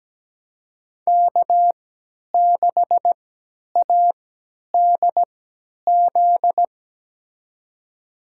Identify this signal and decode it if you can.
Morse code, heard as a keyed tone, about 17 wpm, spaced as for 10 wpm, 700 Hz. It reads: K6ADZ
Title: USA Call Signs - 5 Characters - 3x Character Spacing - 17wpm